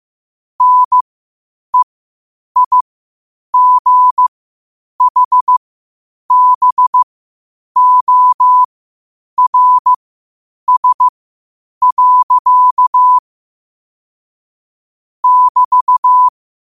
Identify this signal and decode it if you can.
Morse code, heard as a keyed tone, about 15 wpm, spaced as for 8 wpm, 1000 Hz. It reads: NEIGHBORS. <BT>
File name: qotdfiles / 8wpm